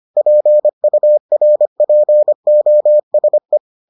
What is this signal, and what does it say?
Morse code, heard as a keyed tone, about 25 wpm, 600 Hz. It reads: PURPOSE